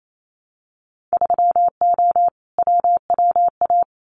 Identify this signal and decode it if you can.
Morse code, heard as a keyed tone, about 28 wpm, 700 Hz. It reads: 3O WWA
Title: Winter Field Day Exchanges Encore 28wpm